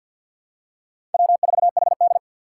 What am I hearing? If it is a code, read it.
Morse code, heard as a keyed tone, about 53 wpm, 700 Hz. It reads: W4HD
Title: Four Challenging Letter-Number Combinations 50wpm